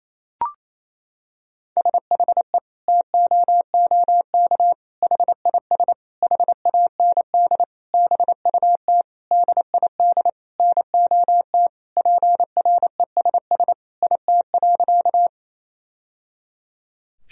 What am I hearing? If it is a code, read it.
Morse code, heard as a keyed tone, about 28 wpm, 700 Hz. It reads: SHE TOOK HIS HAND BUT DID NOT PRESS IT.
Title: Sentences from Top 1000 Words - Review Part 1 - 28wpm